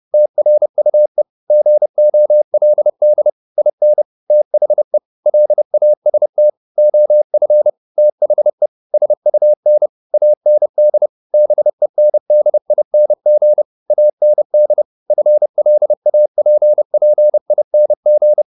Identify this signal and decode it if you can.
Morse code, heard as a keyed tone, about 30 wpm, 600 Hz. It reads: TRUE GOLD IN THE LAST OF THE SUN AND BENDING AND FLAPPING